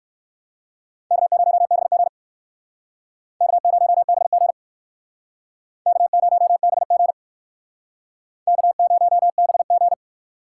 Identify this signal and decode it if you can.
Morse code, heard as a keyed tone, about 56 wpm, 700 Hz. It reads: K0BG K0BG K0BG K0BG